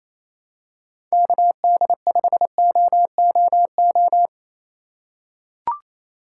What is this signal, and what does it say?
Morse code, heard as a keyed tone, about 28 wpm, 700 Hz. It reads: KD5OOO
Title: USA Call Signs - 6 Characters - 3x Character Spacing - 28wpm